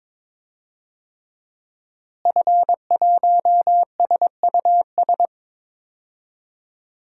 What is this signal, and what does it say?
Morse code, heard as a keyed tone, about 22 wpm, 700 Hz. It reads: F1SUS
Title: DXCC-100 Call Signs - 5 Characters - Repeated 3x - 22wpm